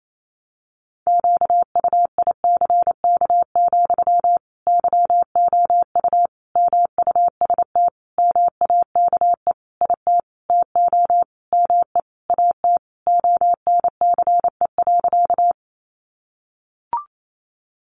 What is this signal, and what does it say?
Morse code, heard as a keyed tone, about 28 wpm, 700 Hz. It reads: QUICK, YOU MUST MAKE IT TO ME AT ONCE.
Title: Sentences from Top 500 Words 28wpm